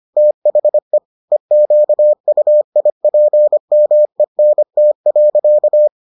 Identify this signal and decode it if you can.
Morse code, heard as a keyed tone, about 25 wpm, 600 Hz. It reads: THE EQUIPMENT.